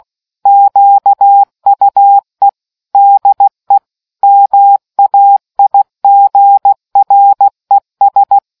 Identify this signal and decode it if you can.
Morse code, heard as a keyed tone, about 16 wpm, 800 Hz. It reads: QUE DE MAIGRES